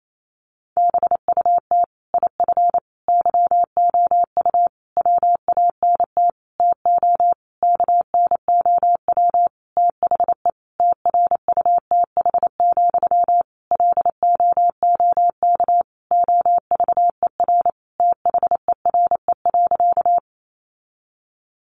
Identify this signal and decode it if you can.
Morse code, heard as a keyed tone, about 28 wpm, 700 Hz. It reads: BUT IF YOU WANT TO KNOW THE TRUTH, LOOK OVER THERE.
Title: Sentences from Top 800 Words 28wpm